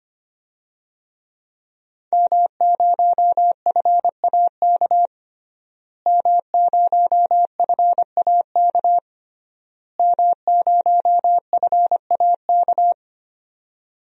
Morse code, heard as a keyed tone, about 25 wpm, 700 Hz. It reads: M0FAK M0FAK M0FAK
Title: DXCC-100 Call Signs - 5 Characters - Repeated 3x - 25wpm